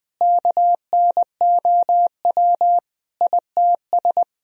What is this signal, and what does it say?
Morse code, heard as a keyed tone, about 21 wpm, 700 Hz. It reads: KNOW ITS
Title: Sentences from Top 1000 Words - Review Part 1 - 20wpm